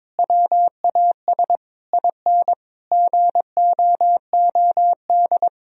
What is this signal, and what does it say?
Morse code, heard as a keyed tone, about 22 wpm, 700 Hz. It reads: WAS IN GOOD
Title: Sentences from Top 1000 Words - Review Part 3 - 22wpm